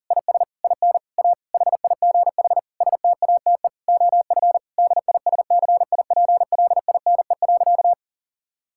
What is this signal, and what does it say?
Morse code, heard as a keyed tone, about 40 wpm, 700 Hz. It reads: IS IN A HIGH STATE OF DISCIPLINE.
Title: Sentences from Top 1000 Words - Review Part 6 - 40wpm